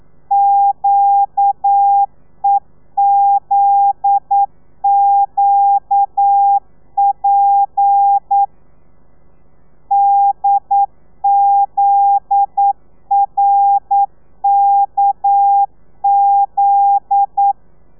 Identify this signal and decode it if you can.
Morse code, heard as a keyed tone, about 9 wpm, 800 Hz. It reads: QEZQP DZRKZ
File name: telegrafia / i2tfj / Lez9 40